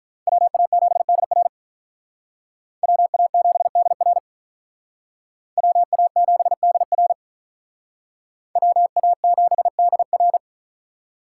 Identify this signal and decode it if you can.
Morse code, heard as a keyed tone, about 44 wpm, 700 Hz. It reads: WA7DR WA7DR WA7DR WA7DR
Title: Speed Racing - USA 2x2 Call Signs 35wpm